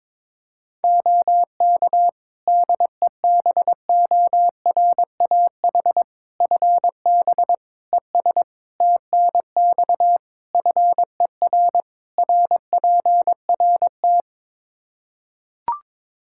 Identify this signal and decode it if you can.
Morse code, heard as a keyed tone, about 22 wpm, 700 Hz. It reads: OK DEBORAH FB ES TNX FER RPRT
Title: Practice QSO - Exchange 2 - Info - 22wpm